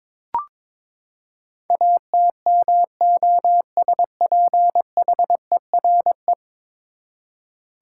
Morse code, heard as a keyed tone, about 22 wpm, 700 Hz. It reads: ATMOSPHERE
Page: morse-code-ninja-practice-22wpm